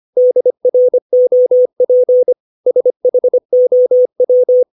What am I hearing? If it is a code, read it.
Morse code, heard as a keyed tone, about 25 wpm, 500 Hz. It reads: DROP SHOW
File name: 632 words / 632 Words 25WPM